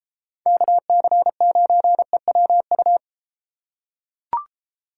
Morse code, heard as a keyed tone, about 33 wpm, 700 Hz. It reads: KC9EWU